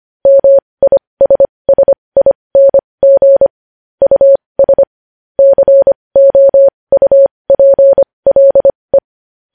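Morse code, heard as a keyed tone, about 25 wpm, spaced as for 20 wpm, 550 Hz. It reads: MISSING US COUPLE
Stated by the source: Morse code news headlines sent at 20 wpm.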